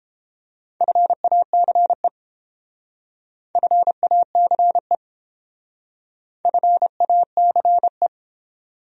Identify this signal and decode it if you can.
Morse code, heard as a keyed tone, about 29 wpm, 700 Hz. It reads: FACE FACE FACE